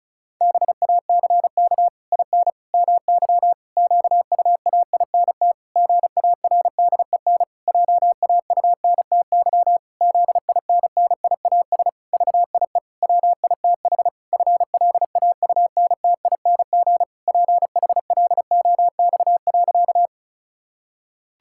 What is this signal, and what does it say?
Morse code, heard as a keyed tone, about 34 wpm, 700 Hz. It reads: BACK IN MY QUAINT GARDEN JAUNTY ZINNIAS VIE WITH FLAUNTING PHLOX.
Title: Pangram Challenge v2 - 35wpm